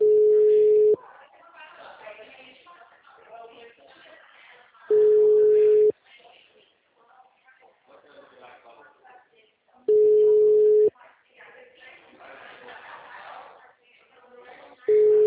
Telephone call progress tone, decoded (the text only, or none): ringback tone